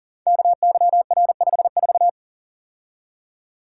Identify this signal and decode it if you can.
Morse code, heard as a keyed tone, about 38 wpm, 700 Hz. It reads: KYR54
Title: Code Groups - 5 Letter:Numbers - 40wpm